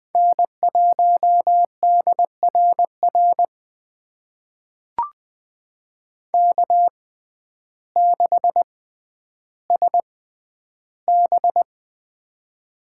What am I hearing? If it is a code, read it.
Morse code, heard as a keyed tone, about 21 wpm, spaced as for 11 wpm, 700 Hz. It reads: N1DRR K6SB
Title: USA Call Signs - 6x Character Spacing - 20wpm